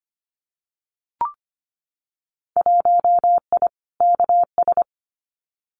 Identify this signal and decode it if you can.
Morse code, heard as a keyed tone, about 25 wpm, 700 Hz. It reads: 1I KS
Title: Winter Field Day Exchanges Encore 25wpm